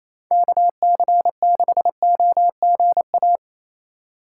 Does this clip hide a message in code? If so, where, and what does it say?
Yes: Morse code from 0.3 to 3.4 s, heard as a keyed tone, about 36 wpm, 700 Hz. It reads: KC6OGA